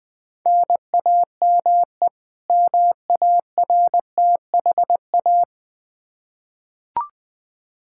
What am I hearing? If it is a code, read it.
Morse code, heard as a keyed tone, about 21 wpm, 700 Hz. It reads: NAME MARTHA
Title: Practice QSO - Exchange 1 - Beginning - 20wpm